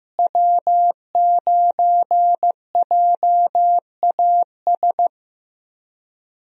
Morse code, heard as a keyed tone, about 15 wpm, 700 Hz. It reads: W9JAS
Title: USA Call Signs - 5 Characters - Repeated 3x - 15wpm